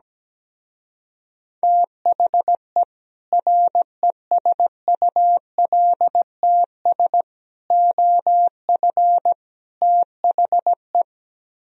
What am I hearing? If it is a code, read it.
Morse code, heard as a keyed tone, about 17 wpm, 700 Hz. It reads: THE RESULTS OF THE
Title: Speed-Racing - Sets of 4 Words 17wpm